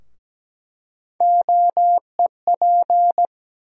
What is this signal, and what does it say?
Morse code, heard as a keyed tone, about 17 wpm, 700 Hz. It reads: OEP
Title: USA Call Sign Postfixes - 3 Letters 17wpm